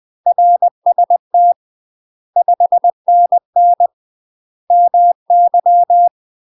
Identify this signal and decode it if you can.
Morse code, heard as a keyed tone, about 20 wpm, 700 Hz. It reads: RST 5NN MY
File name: UK MASTER QSOs TRAINING / 20WPM / UK MASTER QSOs 0025